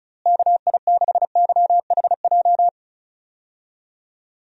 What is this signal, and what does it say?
Morse code, heard as a keyed tone, about 36 wpm, 700 Hz. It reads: KI6YHJ
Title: USA Call Signs - 6 Characters - Repeated 2x - 35wpm